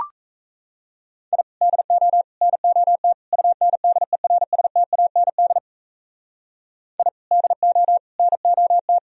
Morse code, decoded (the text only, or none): I DO NOT UNDERSTAND I DO NOT